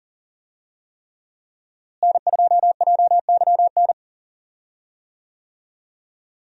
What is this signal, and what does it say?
Morse code, heard as a keyed tone, about 40 wpm, 700 Hz. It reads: N2JYN
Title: USA Call Signs - 5 Characters - Repeated 3x - 40wpm